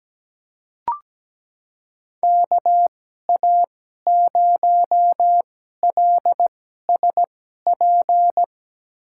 Morse code, heard as a keyed tone, about 17 wpm, spaced as for 12 wpm, 700 Hz. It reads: KA0LSP
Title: USA Call Signs - 6 Characters - 2x Character Spacing - 17wpm